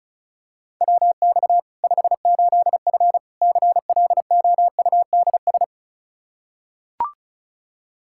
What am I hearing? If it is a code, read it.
Morse code, heard as a keyed tone, about 36 wpm, 700 Hz. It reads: WX 58F CLOUDS
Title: Practice QSO - Exchange 2 - Info - 35wpm